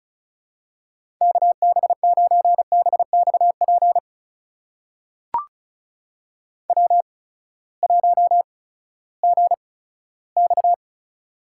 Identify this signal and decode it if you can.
Morse code, heard as a keyed tone, about 36 wpm, spaced as for 18 wpm, 700 Hz. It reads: KB9BXP W1GX
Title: USA Call Signs - 8x Character Spacing - 35wpm